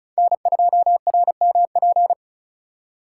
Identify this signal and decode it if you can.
Morse code, heard as a keyed tone, about 36 wpm, 700 Hz. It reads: N2RMP
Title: USA Call Signs - 5 Characters - 4x Character Spacing - 35wpm